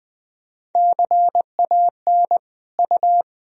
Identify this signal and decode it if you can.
Morse code, heard as a keyed tone, about 21 wpm, 700 Hz. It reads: CAN U
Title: Sentences from Top 400 Words and Abbreviations 20wpm